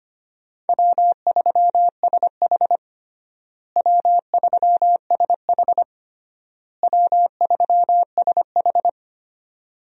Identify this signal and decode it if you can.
Morse code, heard as a keyed tone, about 25 wpm, 700 Hz. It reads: W3SH W3SH W3SH
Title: USA Call Signs - 4 Characters - Repeated 3x - 25wpm